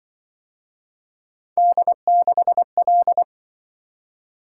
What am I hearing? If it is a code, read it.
Morse code, heard as a keyed tone, about 26 wpm, 700 Hz. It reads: D6L